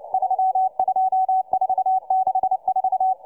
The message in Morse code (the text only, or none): 32464